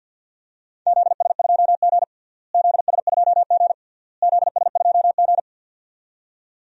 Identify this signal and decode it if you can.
Morse code, heard as a keyed tone, about 52 wpm, 700 Hz. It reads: ZS2G ZS2G ZS2G
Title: DXCC-100 Call Signs - 4 Characters - Repeated 3x - 50wpm